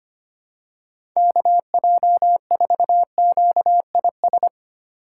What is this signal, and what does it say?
Morse code, heard as a keyed tone, about 25 wpm, 700 Hz. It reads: KJ4QIS
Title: USA Call Signs - 4x Character Spacing - 25wpm